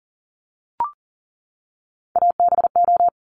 Morse code, heard as a keyed tone, about 41 wpm, 700 Hz. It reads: ABO
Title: Top Three Letter Combinations in Words 40wpm